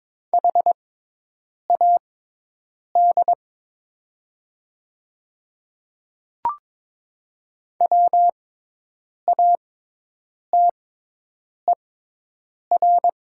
Morse code, heard as a keyed tone, about 22 wpm, spaced as for 8 wpm, 700 Hz. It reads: HAD WATER
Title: 6x Character Spacing - Top 500 Words - 22wpm